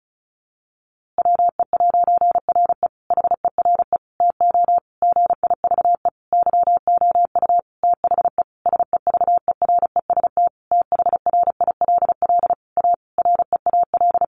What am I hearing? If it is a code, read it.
Morse code, heard as a keyed tone, about 34 wpm, 700 Hz. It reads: WE'RE HERE TO GIVE YOU THE SEVEREST THRILL A REAL
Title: A Sound Of Thunder 35wpm